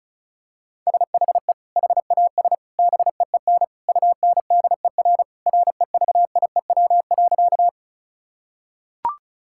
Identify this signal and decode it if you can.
Morse code, heard as a keyed tone, about 34 wpm, 700 Hz. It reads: SHE HAS BEEN UNDER REVIEW.
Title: Sentences from Top 1000 Words - Review Part 6 - 35wpm